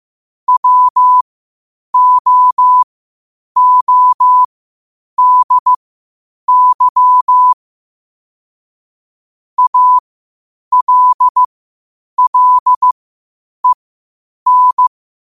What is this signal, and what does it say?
Morse code, heard as a keyed tone, about 15 wpm, spaced as for 8 wpm, 1000 Hz. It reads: WOODY ALLEN
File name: qotdfiles / 8wpm